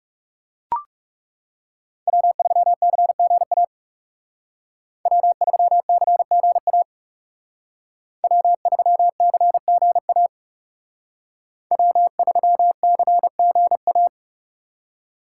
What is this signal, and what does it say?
Morse code, heard as a keyed tone, about 36 wpm, 700 Hz. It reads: W3CGA W3CGA W3CGA W3CGA
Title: Speed Racing - USA 1x3 Call Signs 30wpm